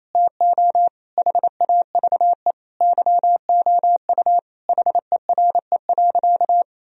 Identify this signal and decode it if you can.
Morse code, heard as a keyed tone, about 28 wpm, 700 Hz. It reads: TO HAVE YOU HERE.
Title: Sentences from Top 1000 Words - Review Part 2 - 28wpm